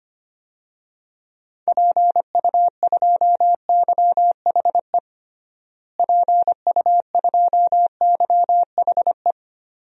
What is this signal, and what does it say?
Morse code, heard as a keyed tone, about 25 wpm, 700 Hz. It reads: PU2YHE PU2YHE
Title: DXCC-100 Call Signs - 6 Characters - Repeated 2x - 25wpm